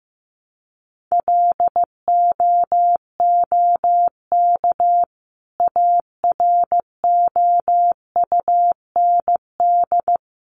Morse code, heard as a keyed tone, about 15 wpm, 700 Hz. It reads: LOOK AROUND
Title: Sentences from Top 600 Words 15wpm